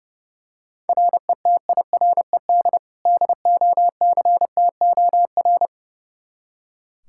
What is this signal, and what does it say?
Morse code, heard as a keyed tone, about 30 wpm, 700 Hz. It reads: RETIRED DOCTOR
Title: Practice QSO - Exchange 3 - More Info - 30wpm